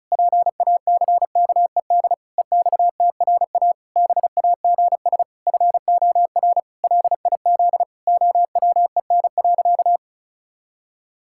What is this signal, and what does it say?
Morse code, heard as a keyed tone, about 34 wpm, 700 Hz. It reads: PACKED EXTRA BAGS FOR LIZ OWEN.
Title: Pangram Challenge v2 - 35wpm